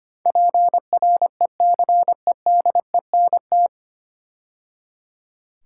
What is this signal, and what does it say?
Morse code, heard as a keyed tone, about 25 wpm, 700 Hz. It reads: PRECEDENT
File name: practice / 25wpm / 9 Letter Words - First 3 Letters Given - 25wpm